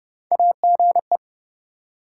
Morse code, heard as a keyed tone, about 30 wpm, 700 Hz. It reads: AGE
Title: Sets of 3 Words - New Word Drill - 30wpm